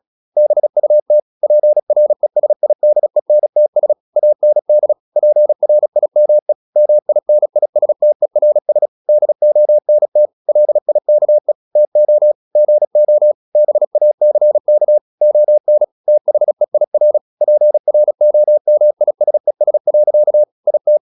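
Morse code, heard as a keyed tone, about 36 wpm, 600 Hz. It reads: BUT PRESIDENTS AND PRIME MINISTERS DONT LIKE TO GO BACK ON THEIR PROMISES. IT
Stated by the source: Morse practice at 36 wpm, text of Ken Follett's novel Never, from CW med Gnister.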